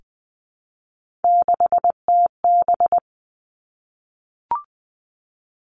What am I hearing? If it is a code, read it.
Morse code, heard as a keyed tone, about 20 wpm, 700 Hz. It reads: 6TB
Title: Three Challenging Letter-Number Combinations 20wpm